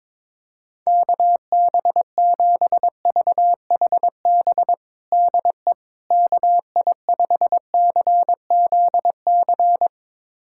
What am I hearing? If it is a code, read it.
Morse code, heard as a keyed tone, about 22 wpm, 700 Hz. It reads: KB7VHB DE KI5CZC
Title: USA Callsign DE Callsign - 6 Characters - 22wpm